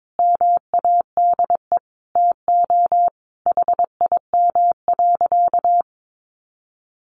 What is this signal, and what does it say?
Morse code, heard as a keyed tone, about 22 wpm, 700 Hz. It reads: MADE TO HIM.
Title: Sentences from Top 1300 Words 22wpm